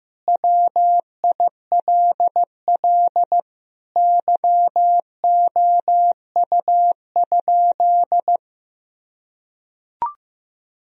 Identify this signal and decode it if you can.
Morse code, heard as a keyed tone, about 15 wpm, 700 Hz. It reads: WILL YOU?
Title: Sentences from Top 1000 Words - Review Part 1 - 15wpm